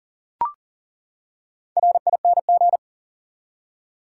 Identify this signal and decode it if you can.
Morse code, heard as a keyed tone, about 40 wpm, 700 Hz. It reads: RING
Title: Four Letter Words - 40wpm